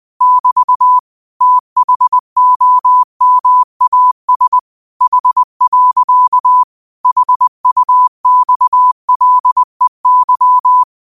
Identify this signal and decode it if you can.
Morse code, heard as a keyed tone, about 20 wpm, 1000 Hz. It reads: <BT> THOMAS H. HUXLEY